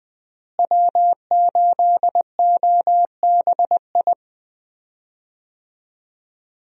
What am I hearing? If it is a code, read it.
Morse code, heard as a keyed tone, about 20 wpm, 700 Hz. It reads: W8OBI
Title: USA Call Signs - 5 Characters - Repeated 2x - 20wpm